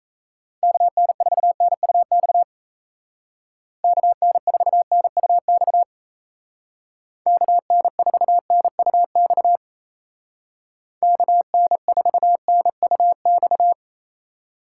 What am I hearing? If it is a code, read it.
Morse code, heard as a keyed tone, about 34 wpm, 700 Hz. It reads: KN4NUX KN4NUX KN4NUX KN4NUX
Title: Speed Racing - USA 2x3 Call Signs 28wpm